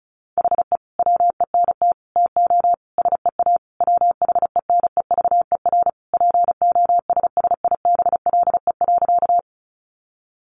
Morse code, heard as a keyed tone, about 34 wpm, 700 Hz. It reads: HE WENT TO SEA WHENEVER POSSIBLE.
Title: Sentences from Top 1200 Words 35wpm